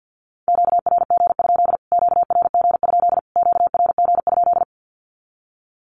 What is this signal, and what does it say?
Morse code, heard as a keyed tone, about 50 wpm, 700 Hz. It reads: QRG? QRG? QRG?
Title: Q-Codes Repeated 3x 50wpm